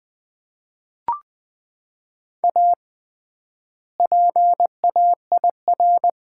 Morse code, heard as a keyed tone, about 20 wpm, 700 Hz. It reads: A PAIR
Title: 3x Word-Spacing - Sets of 3 Words - 20wpm